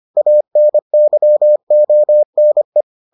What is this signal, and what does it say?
Morse code, heard as a keyed tone, about 25 wpm, 600 Hz. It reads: ANYONE